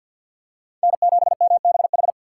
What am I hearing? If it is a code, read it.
Morse code, heard as a keyed tone, about 51 wpm, 690 Hz. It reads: N7MBH
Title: USA Call Signs - 5 Characters - 4x Character Spacing - 50wpm